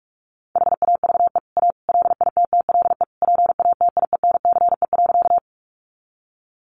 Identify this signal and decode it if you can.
Morse code, heard as a keyed tone, about 44 wpm, 700 Hz. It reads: HAVE A LITTLE PATIENCE.
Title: Sentences from Top 1300 Words 45wpm